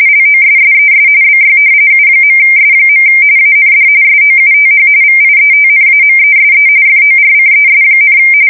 RTTY, about 45.45 baud. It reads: LOSSES BY U.S. AND SOUTH VIETNAMESE FORCES,
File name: fsk170 / battleshipnj